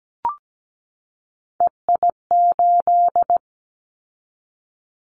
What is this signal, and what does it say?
Morse code, heard as a keyed tone, about 17 wpm, 700 Hz. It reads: EI8
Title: Callsign Prefixes for Top 50 DXCC - 17wpm